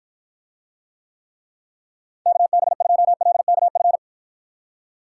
Morse code, heard as a keyed tone, about 48 wpm, 700 Hz. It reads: KB2LKF